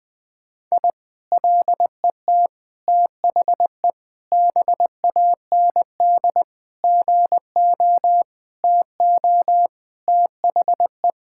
Morse code, heard as a keyed tone, about 21 wpm, 700 Hz. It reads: I LET THE BAND GO TO THE
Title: Sentences from Top 1600 Words 20wpm